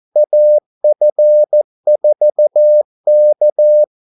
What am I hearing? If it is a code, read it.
Morse code, heard as a keyed tone, about 14 wpm, 600 Hz. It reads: AF4K
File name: Callsigns-14wpm